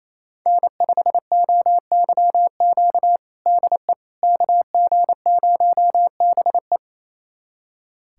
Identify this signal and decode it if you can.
Morse code, heard as a keyed tone, about 28 wpm, 700 Hz. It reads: N5OYQ DE KG0BE
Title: USA Callsign DE Callsign 28wpm